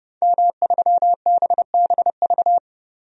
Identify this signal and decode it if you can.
Morse code, heard as a keyed tone, about 30 wpm, 700 Hz. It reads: M3BBV
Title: DXCC-100 Call Signs - 5 Characters - 30wpm